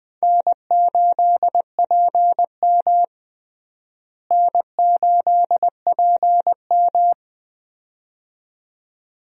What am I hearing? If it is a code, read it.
Morse code, heard as a keyed tone, about 20 wpm, 700 Hz. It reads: N8PM N8PM
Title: USA Call Signs - 4 Characters - Repeated 2x - 20wpm